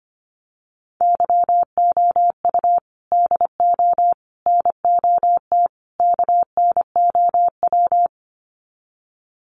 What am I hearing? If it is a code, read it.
Morse code, heard as a keyed tone, about 25 wpm, 700 Hz. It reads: YOU DO NOT KNOW
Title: Sets of 4 Words 25wpm